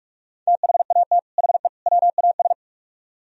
Morse code, heard as a keyed tone, about 59 wpm, spaced as for 52 wpm, 700 Hz. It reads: THAT HE WAS
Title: Speed-Racing - Sets of 3 Words 45wpm